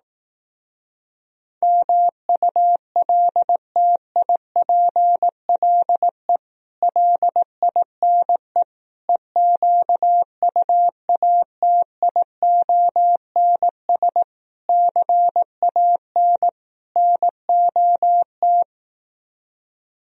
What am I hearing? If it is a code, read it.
Morse code, heard as a keyed tone, about 18 wpm, 700 Hz. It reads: MULTIPLE LINE EQUATIONS CAN NOT
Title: Speed-Racing - Sets of 5 Words 15wpm